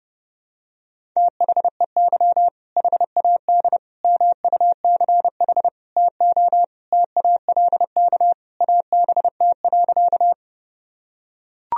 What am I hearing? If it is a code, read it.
Morse code, heard as a keyed tone, about 32 wpm, 700 Hz. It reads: THEY HAD MUCH TO TALK ABT.
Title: Sentences from Top 400 Words and Abbreviations 30wpm